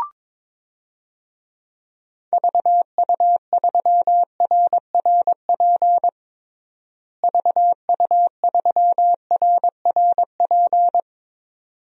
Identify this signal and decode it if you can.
Morse code, heard as a keyed tone, about 22 wpm, 700 Hz. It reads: VU3RRP VU3RRP
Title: DXCC-100 Call Signs - 6 Characters - Repeated 2x - 22wpm